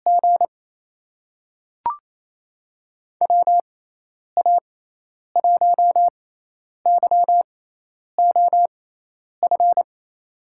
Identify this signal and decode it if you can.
Morse code, heard as a keyed tone, about 28 wpm, spaced as for 10 wpm, 700 Hz. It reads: G WA1YOF
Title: USA Call Signs - 6x Character Spacing - 28wpm